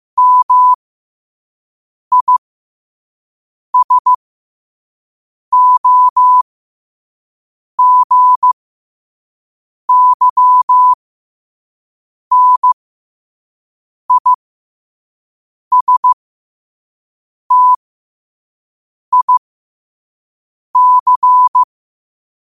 Morse code, heard as a keyed tone, about 15 wpm, spaced as for 5 wpm, 1000 Hz. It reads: MISOGYNISTIC
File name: qotdfiles / 5wpm